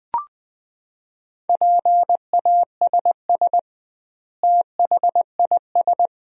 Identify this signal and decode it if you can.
Morse code, heard as a keyed tone, about 20 wpm, 700 Hz. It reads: PASS THIS
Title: 2x Word-Spacing - Sentences from Top 400 Words - 20wpm